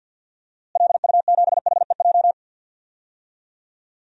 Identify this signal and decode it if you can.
Morse code, heard as a keyed tone, about 51 wpm, 700 Hz. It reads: LU7HEJ
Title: DXCC-100 Call Signs - 6 Characters - Repeated 3x - 50wpm